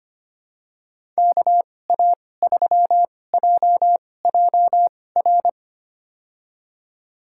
Morse code, heard as a keyed tone, about 25 wpm, spaced as for 18 wpm, 700 Hz. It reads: KA3JJR
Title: USA Call Signs - 6 Characters - 2x Character Spacing - 25wpm